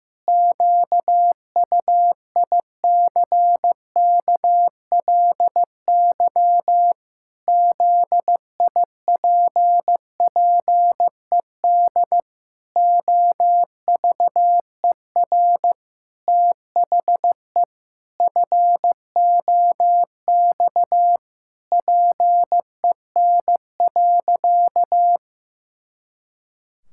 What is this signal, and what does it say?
Morse code, heard as a keyed tone, about 15 wpm, 700 Hz. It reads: QUICKLY ZIPPED OVER THE FOX PEN.
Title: Pangram Challenge v2 - 15wpm